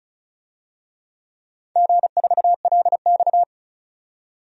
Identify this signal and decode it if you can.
Morse code, heard as a keyed tone, about 34 wpm, 700 Hz. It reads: G4LX
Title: DXCC-100 Call Signs - 4 Characters - 35wpm